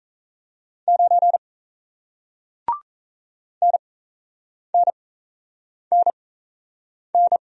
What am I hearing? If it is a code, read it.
Morse code, heard as a keyed tone, about 36 wpm, spaced as for 8 wpm, 700 Hz. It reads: 9 NNNN